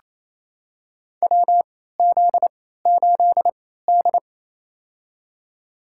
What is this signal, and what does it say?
Morse code, heard as a keyed tone, about 28 wpm, spaced as for 16 wpm, 700 Hz. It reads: WZ8D
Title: USA Call Signs - 4 Characters - 3x Character Spacing - 28wpm